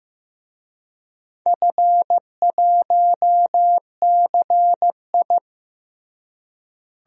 Morse code, heard as a keyed tone, about 15 wpm, 700 Hz. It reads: F1CI
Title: DXCC-100 Call Signs - 4 Characters - Repeated 3x - 15wpm